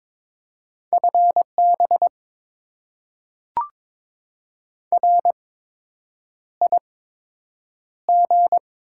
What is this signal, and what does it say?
Morse code, heard as a keyed tone, about 22 wpm, spaced as for 8 wpm, 700 Hz. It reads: FB RIG
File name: Top 20 QSO Elements - 8x Character Spacing - 22wpm